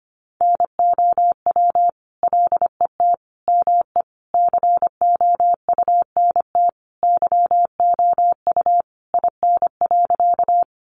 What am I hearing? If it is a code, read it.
Morse code, heard as a keyed tone, about 25 wpm, 700 Hz. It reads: NOW LET ME COUNT YOU IN.
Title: Sentences from Top 1500 Words 25wpm